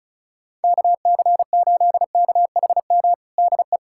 Morse code, heard as a keyed tone, about 35 wpm, 700 Hz. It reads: KC8KHM DE
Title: USA Callsign DE Callsign - 6 Characters - 35wpm